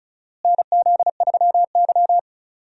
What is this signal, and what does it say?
Morse code, heard as a keyed tone, about 36 wpm, 700 Hz. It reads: NZ3Y
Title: USA Call Signs - 4 Characters - Repeated 2x - 35wpm